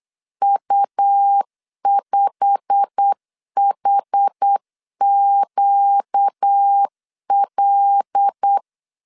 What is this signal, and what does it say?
Morse code, heard as a keyed tone, about 8 wpm, 800 Hz. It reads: U5HQL
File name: CW / DET / 1 9 1500C S A 8wpm 5637